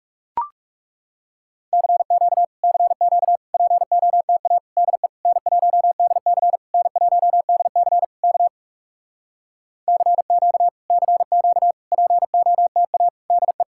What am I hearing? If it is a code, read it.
Morse code, heard as a keyed tone, about 40 wpm, 700 Hz. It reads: CQ CQ POTA DE N1DC N1DC K CQ CQ POTA DE